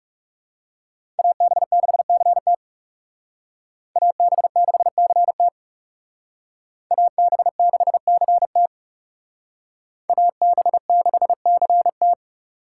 Morse code, heard as a keyed tone, about 38 wpm, 700 Hz. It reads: AB6CT AB6CT AB6CT AB6CT